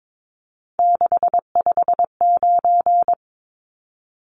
Morse code, heard as a keyed tone, about 22 wpm, 700 Hz. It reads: 659
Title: Numbers - 3-Digit 22wpm